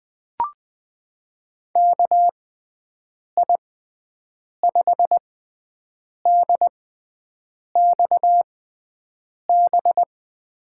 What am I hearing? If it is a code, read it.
Morse code, heard as a keyed tone, about 20 wpm, spaced as for 7 wpm, 700 Hz. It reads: KI5DXB
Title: USA Call Signs - 6 Characters - 6x Character Spacing - 20wpm